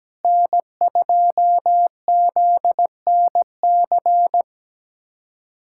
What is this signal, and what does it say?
Morse code, heard as a keyed tone, about 17 wpm, 700 Hz. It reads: N2ZNC
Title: USA Callsign DE Callsign 17wpm